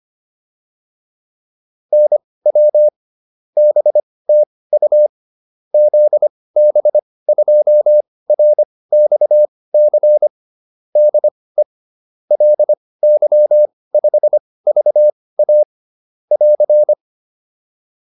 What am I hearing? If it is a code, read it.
Morse code, heard as a keyed tone, about 25 wpm, spaced as for 18 wpm, 600 Hz. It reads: NW BTU ZB2RXC DE LY5VA <AR>